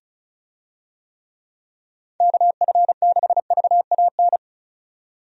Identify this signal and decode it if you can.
Morse code, heard as a keyed tone, about 36 wpm, 700 Hz. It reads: KF6VAN
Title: USA Call Signs - 6 Characters - 35wpm